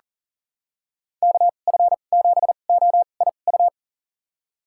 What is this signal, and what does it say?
Morse code, heard as a keyed tone, about 38 wpm, spaced as for 28 wpm, 700 Hz. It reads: KF7OIU
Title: USA Call Signs - 2x Character Spacing - 40wpm